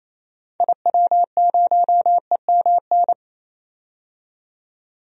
Morse code, heard as a keyed tone, about 28 wpm, 700 Hz. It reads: IW0EMN
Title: DXCC-100 Call Signs - 6 Characters - 28wpm